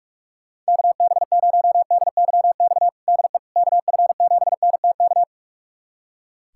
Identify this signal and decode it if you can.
Morse code, heard as a keyed tone, about 46 wpm, 700 Hz. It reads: KB0DYX DE KF7NTK
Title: USA Callsign DE Callsign - 6 Characters - 45wpm